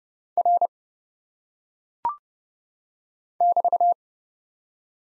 Morse code, heard as a keyed tone, about 30 wpm, 700 Hz. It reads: R <BT>
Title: Top 90 Abbr 30WPM